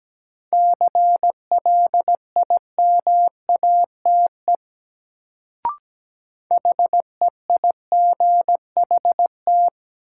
Morse code, heard as a keyed tone, about 17 wpm, 700 Hz. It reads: CLIMATE HEIGHT
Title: Sentences from Top 1600 Words - New Word Drill Alt - 17wpm